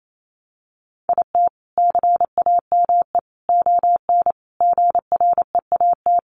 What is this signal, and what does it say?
Morse code, heard as a keyed tone, about 28 wpm, 700 Hz. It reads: IT CAME ON GREAT
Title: A Sound Of Thunder 28wpm